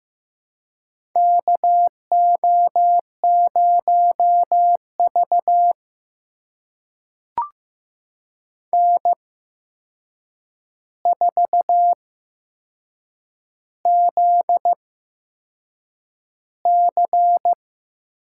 Morse code, heard as a keyed tone, about 15 wpm, spaced as for 6 wpm, 700 Hz. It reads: KO0V N4ZC
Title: USA Call Signs - 4 Characters - 8x Character Spacing - 15wpm